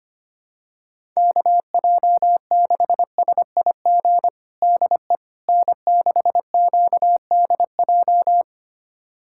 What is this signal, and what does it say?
Morse code, heard as a keyed tone, about 25 wpm, 700 Hz. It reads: KJ6SIG DE N6QDJ
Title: USA Callsign DE Callsign 25wpm